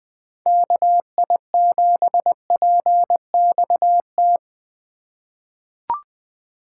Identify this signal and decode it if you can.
Morse code, heard as a keyed tone, about 20 wpm, 700 Hz. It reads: KI7PXT
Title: USA Call Signs - 6 Characters - 4x Character Spacing - 20wpm